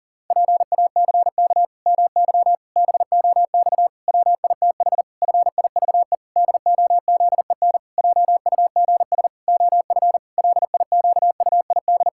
Morse code, heard as a keyed tone, about 40 wpm, 700 Hz. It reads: PACK MY BOX WITH FIVE DOZEN JUGS OF LIQUID
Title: Pangram Challenge v2 - 40wpm